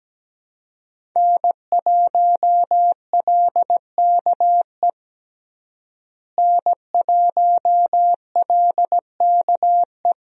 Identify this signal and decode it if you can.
Morse code, heard as a keyed tone, about 17 wpm, 700 Hz. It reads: N1LKE N1LKE
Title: USA Call Signs - 5 Characters - Repeated 2x - 17wpm